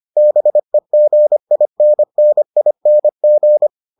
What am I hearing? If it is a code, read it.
Morse code, heard as a keyed tone, about 25 wpm, 600 Hz. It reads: BEGINNING